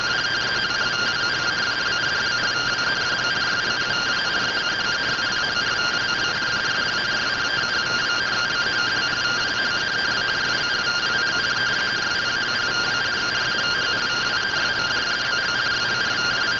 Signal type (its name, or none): two-tone teleprinter signal